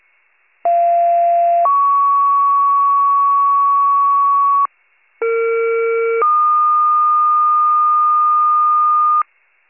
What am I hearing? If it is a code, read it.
Two-tone page, heard as a one-second tone, then a three-second tone, about 690 then 1120 Hz, about 460 then 1220 Hz. It is heard from 0.6 to 9.2 s.